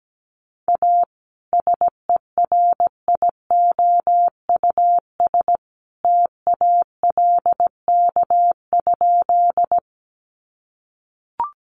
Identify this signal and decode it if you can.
Morse code, heard as a keyed tone, about 17 wpm, 700 Hz. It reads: A SERIOUS TALK?
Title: Sentences from Top 1100 Words 17wpm